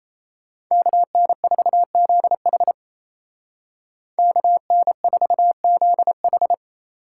Morse code, heard as a keyed tone, about 34 wpm, 700 Hz. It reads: KN4ZH KN4ZH
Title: Speed Racing - USA 2x2 Call Signs 28wpm